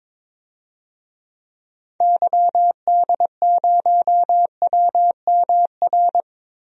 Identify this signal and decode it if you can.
Morse code, heard as a keyed tone, about 22 wpm, 700 Hz. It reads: YD0WMR
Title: DXCC-100 Call Signs - 6 Characters - Repeated 3x - 22wpm